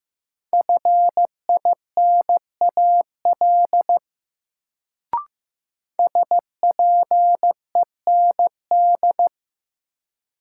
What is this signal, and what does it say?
Morse code, heard as a keyed tone, about 15 wpm, 700 Hz. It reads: FINAL SPEND
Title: Sentences from Top 800 Words - New Word Drill - 15wpm